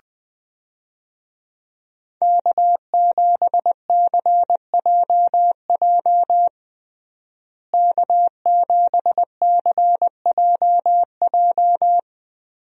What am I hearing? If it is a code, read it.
Morse code, heard as a keyed tone, about 20 wpm, 700 Hz. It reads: K7CJJ K7CJJ
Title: USA Call Signs - 5 Characters - Repeated 3x - 20wpm